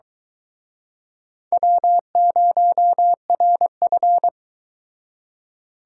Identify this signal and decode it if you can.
Morse code, heard as a keyed tone, about 23 wpm, 700 Hz. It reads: W0RF